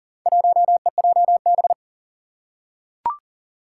Morse code, heard as a keyed tone, about 40 wpm, 700 Hz. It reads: 1EJB
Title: Four Challenging Letter-Number Combinations 40wpm